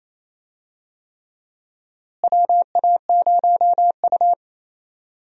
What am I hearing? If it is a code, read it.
Morse code, heard as a keyed tone, about 28 wpm, 700 Hz. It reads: WA0U
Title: USA Call Signs - 4 Characters - Repeated 3x - 28wpm